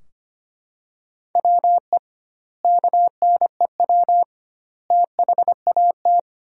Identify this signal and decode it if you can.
Morse code, heard as a keyed tone, about 25 wpm, 700 Hz. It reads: WE KNEW THAT
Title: 2x Word-Spacing - Sentences from Top 400 Words - 25wpm